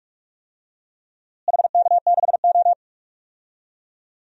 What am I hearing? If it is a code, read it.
Morse code, heard as a keyed tone, about 41 wpm, 700 Hz. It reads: HK6O
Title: DXCC-100 Call Signs - 4 Characters - Repeated 3x - 45wpm